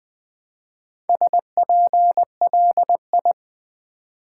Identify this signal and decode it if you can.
Morse code, heard as a keyed tone, about 20 wpm, 700 Hz. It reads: SPLI
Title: Build A Word - 1 Letter At A Time - 3 to 5-Letter Words - 20wpm